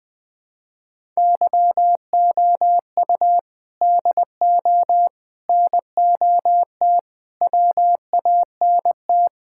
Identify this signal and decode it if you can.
Morse code, heard as a keyed tone, about 20 wpm, 700 Hz. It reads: YOU DO NOT WANT
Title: Sentences with 5-Word Phrases - 20wpm